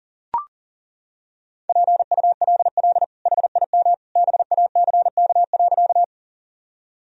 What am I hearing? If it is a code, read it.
Morse code, heard as a keyed tone, about 40 wpm, 700 Hz. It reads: PULL HIM BACK.
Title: Sentences from Top 1000 Words - Review Part 2 - 40wpm